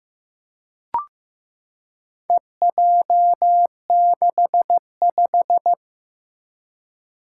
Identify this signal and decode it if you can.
Morse code, heard as a keyed tone, about 15 wpm, 700 Hz. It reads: EJ65
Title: Four Challenging Letter-Number Combinations 15wpm